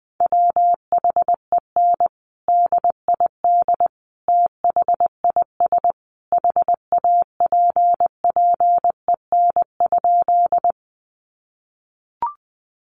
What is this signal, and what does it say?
Morse code, heard as a keyed tone, about 20 wpm, 700 Hz. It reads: WHEN DID THIS HAPPEN?
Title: Sentences from Top 400 Words 20wpm